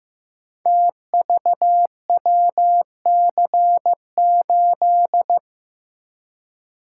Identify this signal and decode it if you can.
Morse code, heard as a keyed tone, about 15 wpm, 700 Hz. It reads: TVWC8
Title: Code Groups - 5 Letter:Numbers - 15wpm